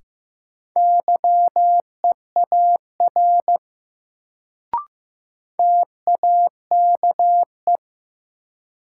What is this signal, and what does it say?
Morse code, heard as a keyed tone, about 16 wpm, 700 Hz. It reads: YEAR TAKE
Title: Sentences from Top 200 Words - New Word Drill - 15wpm